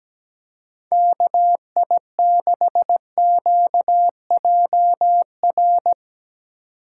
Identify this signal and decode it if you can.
Morse code, heard as a keyed tone, about 17 wpm, 700 Hz. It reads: KI6QJR
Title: USA Call Signs - 6 Characters - 2x Character Spacing - 17wpm